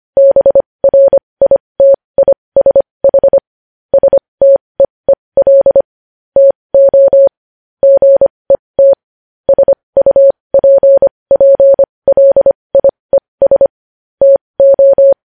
Morse code, heard as a keyed tone, about 25 wpm, spaced as for 20 wpm, 550 Hz. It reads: BRITISH STEEL TO GET SUPPLIES TO
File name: podcast / News headlines-1x-25x20